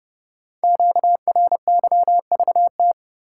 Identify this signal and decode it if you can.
Morse code, heard as a keyed tone, about 28 wpm, 700 Hz. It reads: QRYVT
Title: Code Groups - 5 Letter:Numbers - 30wpm